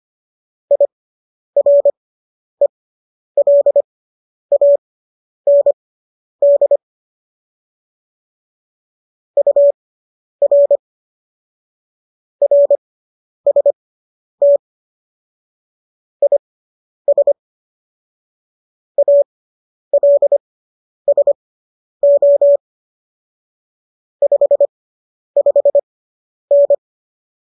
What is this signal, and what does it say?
Morse code, heard as a keyed tone, about 25 wpm, spaced as for 10 wpm, 570 Hz. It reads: IRELAND UR RST IS ALSO 55N